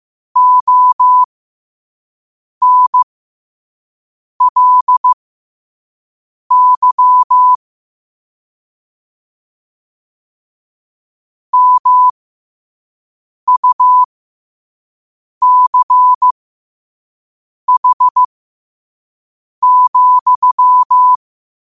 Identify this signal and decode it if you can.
Morse code, heard as a keyed tone, about 15 wpm, spaced as for 5 wpm, 1000 Hz. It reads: ONLY MUCH,
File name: qotdfiles / 5wpm